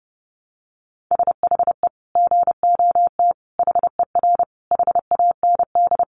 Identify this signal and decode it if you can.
Morse code, heard as a keyed tone, about 28 wpm, 700 Hz. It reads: SHE GOT HER HAND
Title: Sentences from Top 500 Words 30wpm